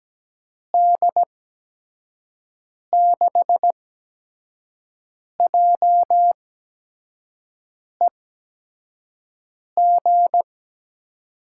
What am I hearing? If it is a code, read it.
Morse code, heard as a keyed tone, about 17 wpm, spaced as for 5 wpm, 700 Hz. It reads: D6JEG
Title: USA Call Signs - 8x Character Spacing - 17wpm